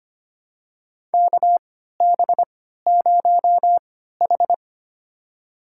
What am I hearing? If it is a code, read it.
Morse code, heard as a keyed tone, about 25 wpm, spaced as for 14 wpm, 700 Hz. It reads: KB0H
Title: USA Call Signs - 4 Characters - 3x Character Spacing - 25wpm